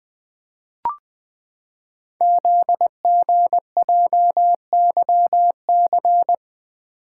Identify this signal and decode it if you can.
Morse code, heard as a keyed tone, about 20 wpm, 700 Hz. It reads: ZGJYC
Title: Code Groups - 5 Letter:Numbers - 20wpm